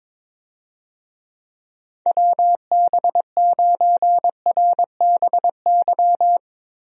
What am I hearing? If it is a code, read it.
Morse code, heard as a keyed tone, about 22 wpm, 700 Hz. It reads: WB9RBY
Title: USA Call Signs - 6 Characters - 22wpm